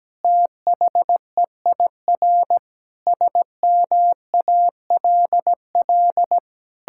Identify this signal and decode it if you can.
Morse code, heard as a keyed tone, about 17 wpm, 700 Hz. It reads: THEIR SMALL
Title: Sentences from Top 1000 Words - Review Part 6 - 17wpm